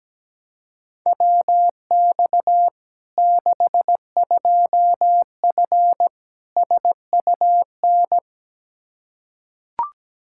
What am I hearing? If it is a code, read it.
Morse code, heard as a keyed tone, about 17 wpm, 700 Hz. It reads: WX 62F SUN
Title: Practice QSO - Exchange 2 - Info - 17wpm